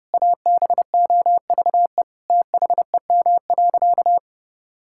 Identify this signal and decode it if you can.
Morse code, heard as a keyed tone, about 32 wpm, 700 Hz. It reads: ABOVE THEM.
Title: Sentences from Top 1000 Words - Review Part 5 - 30wpm